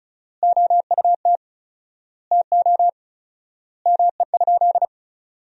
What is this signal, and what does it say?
Morse code, heard as a keyed tone, about 34 wpm, 700 Hz. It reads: OUT TO ME?
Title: 4x Word-Spacing - Sentences from Top 200 Words - 35wpm